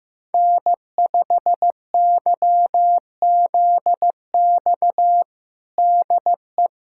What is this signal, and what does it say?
Morse code, heard as a keyed tone, about 16 wpm, 700 Hz. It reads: N5YZX DE
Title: USA Callsign DE Callsign - 5 Characters - 15wpm